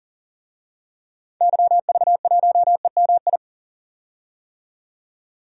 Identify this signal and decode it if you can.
Morse code, heard as a keyed tone, about 40 wpm, 700 Hz. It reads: YV1EMI
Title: DXCC-100 Call Signs - 6 Characters - 40wpm